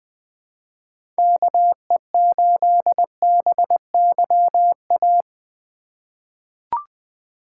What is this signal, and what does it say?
Morse code, heard as a keyed tone, about 20 wpm, 700 Hz. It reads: KE8BYA
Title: USA Call Signs - 6 Characters - 3x Character Spacing - 20wpm